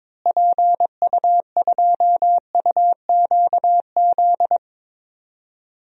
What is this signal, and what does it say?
Morse code, heard as a keyed tone, about 22 wpm, 700 Hz. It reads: PU2UQZ
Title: DXCC-100 Call Signs - 6 Characters - Repeated 3x - 22wpm